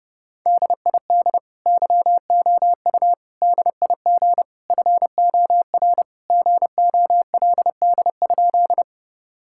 Morse code, heard as a keyed tone, about 30 wpm, 700 Hz. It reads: DID YOU DIG FOR GOLD?
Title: Sentences from Top 1600 Words 30wpm